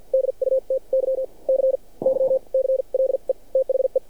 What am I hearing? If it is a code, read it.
Morse code, heard as a keyed tone, about 49 wpm, spaced as for 36 wpm, 530 Hz. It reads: DUTY CYCLE THE
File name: archived / CRSnet